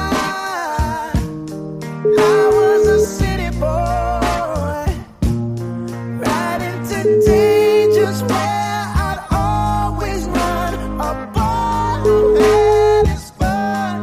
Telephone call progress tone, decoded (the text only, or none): ringback tone